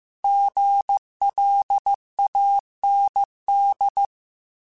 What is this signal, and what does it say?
Morse code, heard as a keyed tone, about 15 wpm, 800 Hz. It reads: GLAND
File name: Alphabetical / Alphabetical mp3s / Gs @13